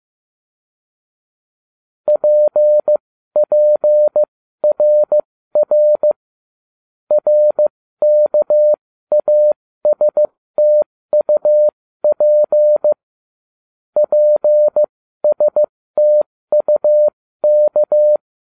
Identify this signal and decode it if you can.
Morse code, heard as a keyed tone, about 15 wpm, spaced as for 13 wpm, 600 Hz. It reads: PPRR RKASTUP PSTUK
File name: cw / ubacw07 1